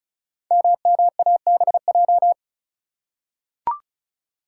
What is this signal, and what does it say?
Morse code, heard as a keyed tone, about 36 wpm, 700 Hz. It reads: MMABJ
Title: Code Groups - 5 Letter:Numbers - 35wpm